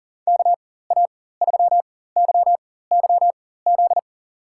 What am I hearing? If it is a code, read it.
Morse code, heard as a keyed tone, about 38 wpm, spaced as for 18 wpm, 700 Hz. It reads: KA3YYZ
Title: USA Call Signs - 6 Characters - 4x Character Spacing - 40wpm